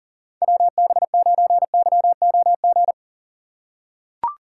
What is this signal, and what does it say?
Morse code, heard as a keyed tone, about 40 wpm, 700 Hz. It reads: WB9YOG
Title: USA Call Signs - 6 Characters - 2x Character Spacing - 40wpm